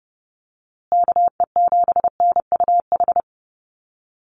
Morse code, heard as a keyed tone, about 30 wpm, 700 Hz. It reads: KE7NUH
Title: USA Call Signs Repeated 3x 30wpm